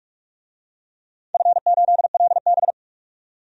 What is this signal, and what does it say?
Morse code, heard as a keyed tone, about 46 wpm, 700 Hz. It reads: F8LB
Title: DXCC-100 Call Signs - 4 Characters - 45wpm